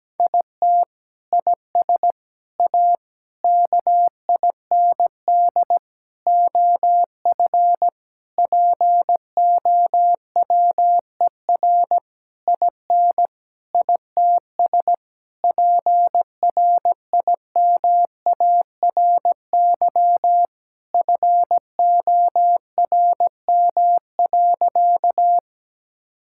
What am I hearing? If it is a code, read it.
Morse code, heard as a keyed tone, about 17 wpm, 700 Hz. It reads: IT IS A KIND OF POWER IN ITS PRIMARY FORM.
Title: Sentences from Top 1000 Words - Review Part 5 - 17wpm